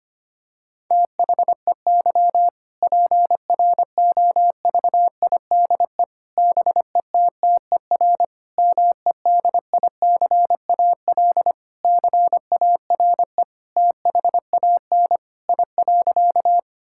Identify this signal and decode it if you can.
Morse code, heard as a keyed tone, about 25 wpm, 700 Hz. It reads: THEY PROVIDE BETTER MEDICAL CARE THAN I.
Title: Sentences from Top 1000 Words - Review Part 5 - 25wpm